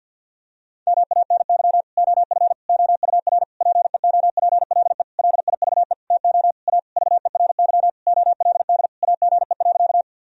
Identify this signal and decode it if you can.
Morse code, heard as a keyed tone, about 50 wpm, 700 Hz. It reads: MANY OF OUR PEOPLE LIVE TO A VERY OLD AGE.
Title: Sentences from Top 1000 Words - Review Part 3 - 50wpm